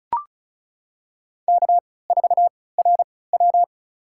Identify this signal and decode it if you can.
Morse code, heard as a keyed tone, about 34 wpm, spaced as for 20 wpm, 700 Hz. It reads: K4RW
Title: USA Call Signs - 4 Characters - 3x Character Spacing - 35wpm